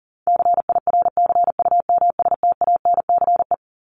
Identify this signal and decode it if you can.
Morse code, heard as a keyed tone, about 37 wpm, 700 Hz. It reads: CIRCUMSTANCE
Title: ES:S Suffix - Top 100 Words with Base Words 40wpm